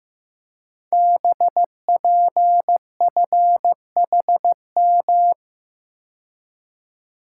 Morse code, heard as a keyed tone, about 15 wpm, 700 Hz. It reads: BPFHM
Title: Code Groups - 5 Letters - 15wpm